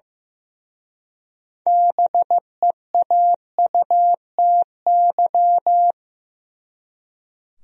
Morse code, heard as a keyed tone, about 18 wpm, 700 Hz. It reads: BEAUTY
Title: Speed-Racing Top 500 Words 15wpm